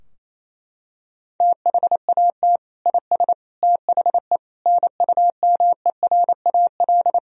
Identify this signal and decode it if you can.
Morse code, heard as a keyed tone, about 28 wpm, 700 Hz. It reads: THAT IS THE NUMERAL
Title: Sentences from Top 1000 Words - Review Part 2 - 28wpm